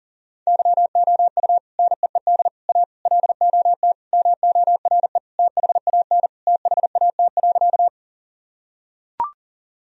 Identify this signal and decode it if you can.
Morse code, heard as a keyed tone, about 37 wpm, 700 Hz. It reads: YOU NEED A LOT MORE THAN THAT.
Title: Sentences from Top 1000 Words - Review Part 3 - 40wpm